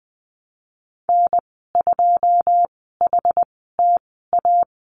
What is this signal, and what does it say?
Morse code, heard as a keyed tone, about 20 wpm, spaced as for 15 wpm, 700 Hz. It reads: N2HTA
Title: 082 Lesson 14g - Number 2 - Call Signs 20wpm